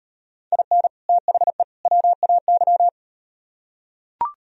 Morse code, heard as a keyed tone, about 38 wpm, 700 Hz. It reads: IN THE WAY